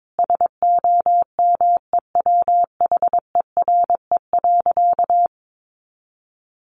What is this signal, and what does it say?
Morse code, heard as a keyed tone, about 22 wpm, 700 Hz. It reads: SOMEWHERE.
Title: Sentences from Top 1400 Words 22wpm